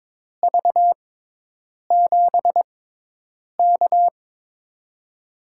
Morse code, heard as a keyed tone, about 22 wpm, spaced as for 8 wpm, 700 Hz. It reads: V7K
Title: USA Call Signs - 4 Characters - 6x Character Spacing - 22wpm